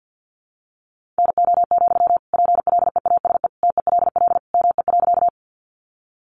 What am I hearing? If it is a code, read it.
Morse code, heard as a keyed tone, about 50 wpm, 700 Hz. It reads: NO, PLEASE TELL ME.
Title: Sentences from Top 1500 Words 50wpm